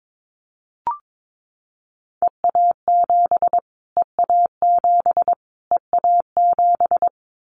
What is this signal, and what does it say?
Morse code, heard as a keyed tone, about 22 wpm, 700 Hz. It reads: EA7 EA7 EA7
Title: Callsign Prefixes for Top 25 DXCC - Repeated 3x - 22wpm